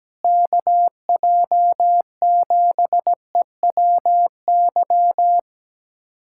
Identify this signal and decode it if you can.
Morse code, heard as a keyed tone, about 17 wpm, 700 Hz. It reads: KJ7EWY
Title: USA Callsign DE Callsign 17wpm